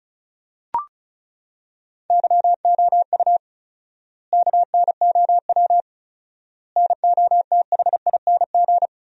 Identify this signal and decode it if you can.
Morse code, heard as a keyed tone, about 34 wpm, 700 Hz. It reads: YOU KNOW NOTHING
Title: 4x Word-Spacing - Sentences from Top 500 Words - 35wpm